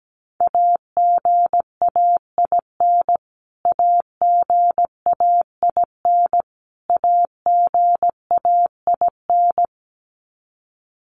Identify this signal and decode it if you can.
Morse code, heard as a keyed tone, about 17 wpm, 700 Hz. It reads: AGAIN AGAIN AGAIN
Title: Top 200 Words Repeated 17wpm